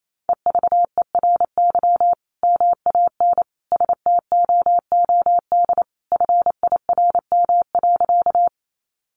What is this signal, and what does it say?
Morse code, heard as a keyed tone, about 28 wpm, 700 Hz. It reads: EVERY MAN STOOD FIRM.
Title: Sentences from Top 1400 Words 28wpm